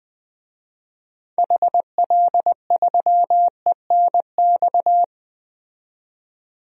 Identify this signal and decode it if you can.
Morse code, heard as a keyed tone, about 20 wpm, 700 Hz. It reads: HL3ENX
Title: DXCC-100 Call Signs - 6 Characters - 20wpm